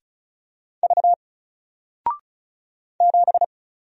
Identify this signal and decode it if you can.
Morse code, heard as a keyed tone, about 36 wpm, 700 Hz. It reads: V 7
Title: Single Character (Letter:Number:Punctuation:Prosign) 35wpm